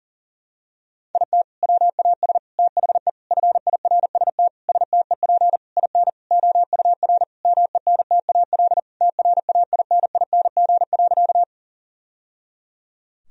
Morse code, heard as a keyed tone, about 40 wpm, 700 Hz. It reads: IT WAS THE FIRST STEP IN OUR MENTAL TRAINING.
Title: Sentences from Top 1000 Words - Review Part 4 - 40wpm